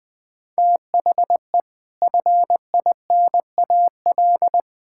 Morse code, heard as a keyed tone, about 21 wpm, 700 Hz. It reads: THE FINAL
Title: Sets of 6 Words - 20wpm